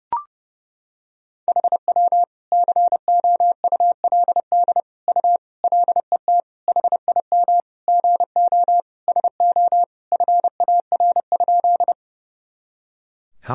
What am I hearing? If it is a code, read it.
Morse code, heard as a keyed tone, about 30 wpm, 700 Hz. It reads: HW COULD U LET HIM GO SO FAR?
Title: Sentences from Top 300 Words and Abbreviations 30wpm